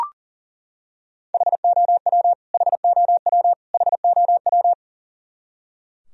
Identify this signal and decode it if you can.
Morse code, heard as a keyed tone, about 40 wpm, 700 Hz. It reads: HOW HOW HOW
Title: Top 100 Words Repeated 40wpm